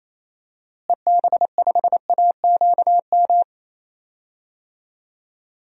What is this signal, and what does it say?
Morse code, heard as a keyed tone, about 28 wpm, 700 Hz. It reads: EB5AQM
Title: DXCC-100 Call Signs - 6 Characters - Repeated 3x - 28wpm